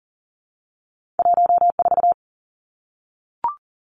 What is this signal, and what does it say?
Morse code, heard as a keyed tone, about 40 wpm, 700 Hz. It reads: 1 4
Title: Two Challenging Letter-Number Combinations 40wpm